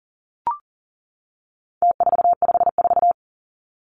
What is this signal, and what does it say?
Morse code, heard as a keyed tone, about 40 wpm, 700 Hz. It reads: T454
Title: Numbers - Contest Style - 4-Digit 40wpm